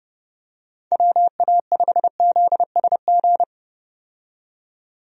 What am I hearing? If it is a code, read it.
Morse code, heard as a keyed tone, about 31 wpm, 700 Hz. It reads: WA5ZSG
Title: USA Call Signs - 6 Characters - 30wpm